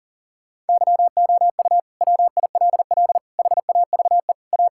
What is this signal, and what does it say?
Morse code, heard as a keyed tone, about 40 wpm, 700 Hz. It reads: YOU WILL HAVE A
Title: Sentences from Top 1000 Words - Review Part 6 - 40wpm